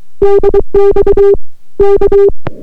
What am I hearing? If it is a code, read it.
Morse code, heard as a keyed tone, about 23 wpm, 400 Hz. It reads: DX K